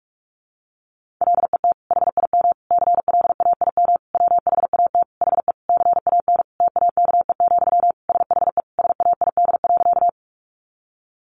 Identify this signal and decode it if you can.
Morse code, heard as a keyed tone, about 44 wpm, 700 Hz. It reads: LET HIM CLAIM WHAT HE CAN TAKE, SHE SAID.
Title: Sentences from Top 1500 Words 45wpm